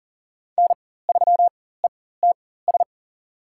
Morse code, heard as a keyed tone, about 40 wpm, spaced as for 19 wpm, 700 Hz. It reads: N3ETS
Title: USA Call Signs - 5 Characters - 4x Character Spacing - 40wpm